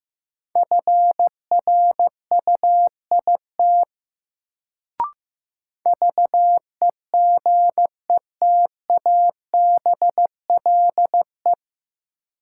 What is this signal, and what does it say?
Morse code, heard as a keyed tone, about 15 wpm, 700 Hz. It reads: FRUIT VEGETABLE
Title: Sentences from Top 1300 Words - New Word Drill - 15wpm